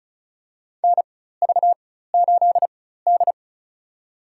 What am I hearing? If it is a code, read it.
Morse code, heard as a keyed tone, about 34 wpm, spaced as for 16 wpm, 700 Hz. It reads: NV8D
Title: USA Call Signs - 4 Characters - 4x Character Spacing - 35wpm